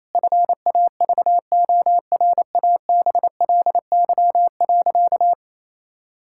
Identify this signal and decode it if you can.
Morse code, heard as a keyed tone, about 28 wpm, 700 Hz. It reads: FAVORABLY.
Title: Pangram Challenge v2 - 28wpm